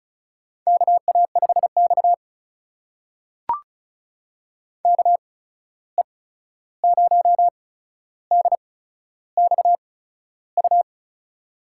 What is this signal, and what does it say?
Morse code, heard as a keyed tone, about 36 wpm, spaced as for 13 wpm, 700 Hz. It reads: KA5X KE0DXU
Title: USA Call Signs - 8x Character Spacing - 35wpm